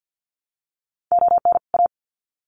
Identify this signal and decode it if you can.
Morse code, heard as a keyed tone, about 50 wpm, 700 Hz. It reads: ON A
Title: Sets of 2 Words 50wpm